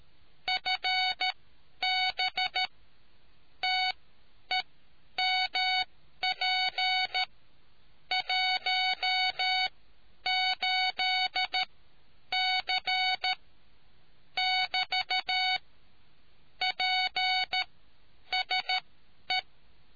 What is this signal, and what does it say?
Morse code, heard as a keyed tone, about 13 wpm, spaced as for 9 wpm, 740 Hz. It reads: FB TEMP 18C <BT> PSE